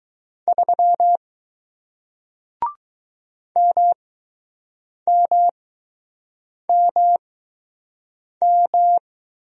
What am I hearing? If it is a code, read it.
Morse code, heard as a keyed tone, about 20 wpm, spaced as for 6 wpm, 700 Hz. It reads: 3 MMMM